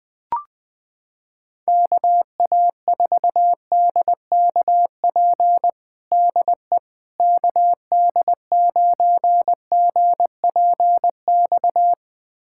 Morse code, heard as a keyed tone, about 21 wpm, 700 Hz. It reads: KA4DKP DE KD9GPX
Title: USA Callsign DE Callsign 20wpm